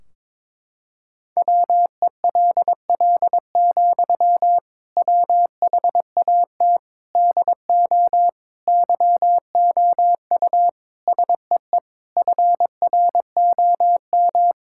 Morse code, heard as a keyed tone, about 22 wpm, 700 Hz. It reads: WELL, WHAT DO YOU SEE FROM
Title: Sentences from Top 1600 Words 22wpm